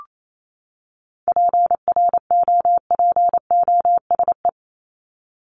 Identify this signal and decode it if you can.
Morse code, heard as a keyed tone, about 28 wpm, 700 Hz. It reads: PROPOSE
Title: ED Suffix - Top 100 Words with Base Words 28wpm